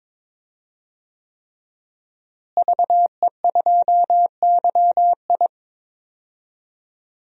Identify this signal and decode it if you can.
Morse code, heard as a keyed tone, about 22 wpm, 700 Hz. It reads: VE2YI
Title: DXCC-100 Call Signs - 5 Characters - Repeated 3x - 22wpm